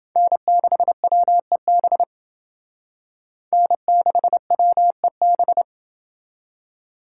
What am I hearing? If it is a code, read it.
Morse code, heard as a keyed tone, about 29 wpm, 700 Hz. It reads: N6WEB N6WEB